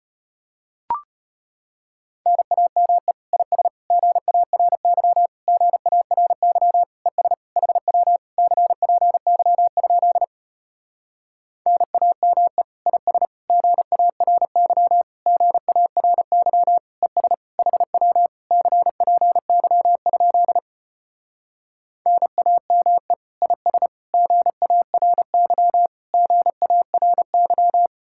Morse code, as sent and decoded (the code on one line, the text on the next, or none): -. .- -- . / .. ... / --. .- .-. -.-- / --. .- .-. -.-- / . ... / .... .-- / -.-. .--. -.-- ..--.. / -. .- -- . / .. ... / --. .- .-. -.-- / --. .- .-. -.-- / . ... / .... .-- / -.-. .--. -.-- ..--.. / -. .- -- . / .. ... / --. .- .-. -.-- / --. .- .-. -.--
NAME IS GARY GARY ES HW CPY? NAME IS GARY GARY ES HW CPY? NAME IS GARY GARY